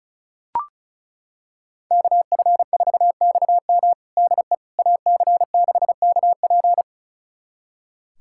Morse code, heard as a keyed tone, about 34 wpm, 700 Hz. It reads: KF4XM DE AC6KP
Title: USA Callsign DE Callsign - 5 Characters - 35wpm